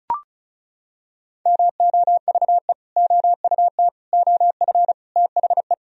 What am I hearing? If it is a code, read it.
Morse code, heard as a keyed tone, about 34 wpm, 700 Hz. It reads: MOVE OUT OF THE
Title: Sentences from Top 1000 Words - Review Part 1 - 35wpm